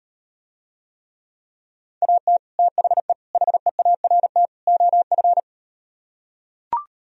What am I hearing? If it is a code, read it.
Morse code, heard as a keyed tone, about 38 wpm, 700 Hz. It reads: AT THE HEART OF